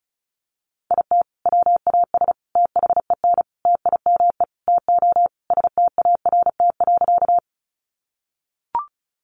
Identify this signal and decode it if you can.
Morse code, heard as a keyed tone, about 34 wpm, 700 Hz. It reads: IT WAS THEN TIME TO START.
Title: Sentences from Top 300 Words 35wpm